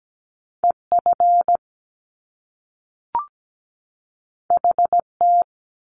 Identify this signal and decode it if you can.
Morse code, heard as a keyed tone, about 17 wpm, 700 Hz. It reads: EF HT
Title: Top Two Letter Combinations in Words 17wpm